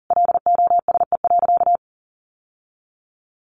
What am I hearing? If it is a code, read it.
Morse code, heard as a keyed tone, about 40 wpm, 700 Hz. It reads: LOSE.
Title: Sentences from Top 1100 Words 40wpm